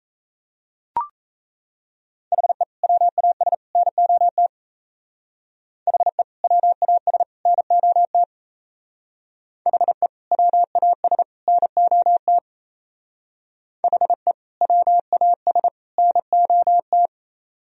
Morse code, heard as a keyed tone, about 34 wpm, 700 Hz. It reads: HE WAS NOT HE WAS NOT HE WAS NOT HE WAS NOT
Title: Speed-Racing - Sets of 3 Words 28wpm